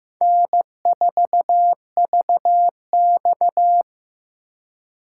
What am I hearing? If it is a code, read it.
Morse code, heard as a keyed tone, about 15 wpm, 700 Hz. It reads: N4VX
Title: USA Call Signs - 4 Characters - 3x Character Spacing - 15wpm